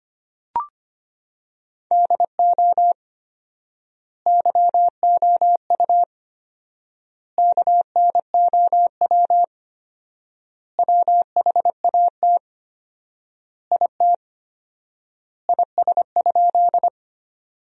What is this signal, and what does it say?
Morse code, heard as a keyed tone, about 25 wpm, 700 Hz. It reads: DO YOU KNOW WHAT IT IS?
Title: 4x Word-Spacing - Sentences from Top 100 Words - 25wpm